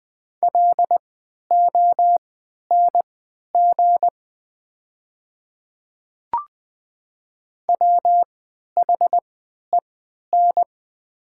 Morse code, heard as a keyed tone, about 20 wpm, spaced as for 11 wpm, 700 Hz. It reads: LONG WHEN
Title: 3x Character Spacing - Top 200 Words - 20wpm